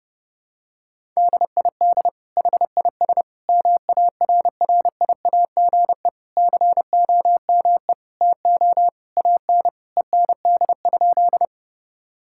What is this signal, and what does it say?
Morse code, heard as a keyed tone, about 30 wpm, 700 Hz. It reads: DID HIS MARRIAGE COME TO AN END?
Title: Sentences from Top 1000 Words - Review Part 5 - 30wpm